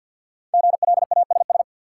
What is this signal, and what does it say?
Morse code, heard as a keyed tone, about 47 wpm, 700 Hz. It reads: GLASS
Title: Sentences from Top 1000 Words - Review Part 3 - New Word Drill - 50wpm